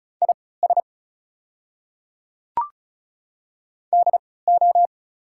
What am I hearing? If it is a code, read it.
Morse code, heard as a keyed tone, about 34 wpm, spaced as for 20 wpm, 700 Hz. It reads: IS DO
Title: 3x Character Spacing - Top 200 Words - 35wpm